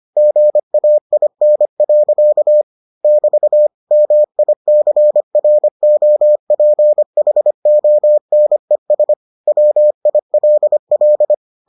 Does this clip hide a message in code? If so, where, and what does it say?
Yes: Morse code from 0.2 to 11.3 s, heard as a keyed tone, about 25 wpm, 600 Hz. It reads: GAIN. <BT> MICROPHONES WILL